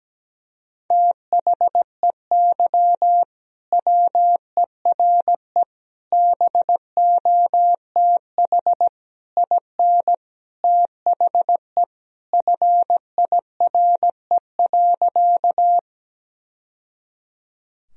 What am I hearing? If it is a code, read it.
Morse code, heard as a keyed tone, about 17 wpm, 700 Hz. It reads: THEY WERE BOTH IN THE FIRE.
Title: Sentences from Top 1000 Words - Review Part 1 - 17wpm